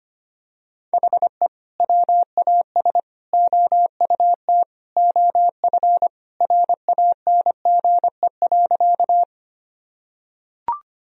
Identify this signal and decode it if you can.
Morse code, heard as a keyed tone, about 25 wpm, 700 Hz. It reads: HE WAS OUT OF RANGE.
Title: Sentences from Top 1000 Words - Review Part 4 - 25wpm